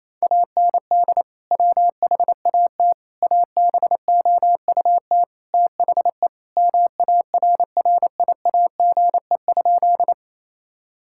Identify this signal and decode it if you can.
Morse code, heard as a keyed tone, about 28 wpm, 700 Hz. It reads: AND WHAT ABOUT THE MARRIAGE?
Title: Sentences from Top 1000 Words - Review Part 5 - 28wpm